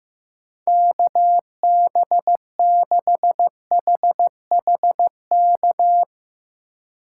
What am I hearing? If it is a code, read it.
Morse code, heard as a keyed tone, about 15 wpm, 700 Hz. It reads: KB6HHK
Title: USA Call Signs - 6 Characters - 4x Character Spacing - 15wpm